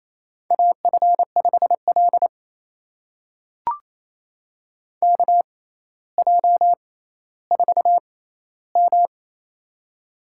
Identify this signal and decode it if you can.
Morse code, heard as a keyed tone, about 27 wpm, spaced as for 14 wpm, 700 Hz. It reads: AF5L KJ4M
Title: USA Call Signs - 4 Characters - 6x Character Spacing - 28wpm